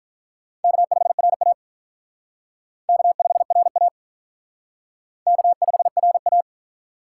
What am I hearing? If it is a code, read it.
Morse code, heard as a keyed tone, about 45 wpm, 700 Hz. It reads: K5RA K5RA K5RA